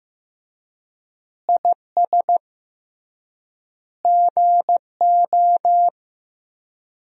Morse code, heard as a keyed tone, about 15 wpm, 700 Hz. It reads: IS GO
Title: 3x Word-Spacing - Sentences from Top 200 Words - 15wpm